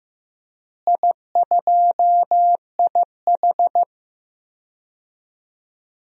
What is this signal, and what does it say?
Morse code, heard as a keyed tone, about 16 wpm, 700 Hz. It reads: I2IH
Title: DXCC-100 Call Signs - 4 Characters - Repeated 2x - 15wpm